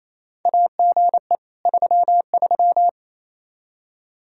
Morse code, heard as a keyed tone, about 27 wpm, 700 Hz. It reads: AGE 33
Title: K1USN SST QSO Practice 28wpm